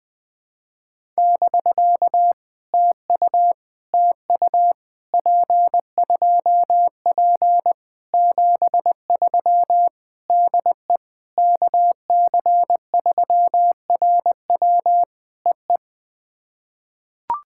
Morse code, heard as a keyed tone, about 20 wpm, 700 Hz. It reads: <BK> TU TU P2P 73 DE KC3RW EE
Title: POTA QSO Practice - 20wpm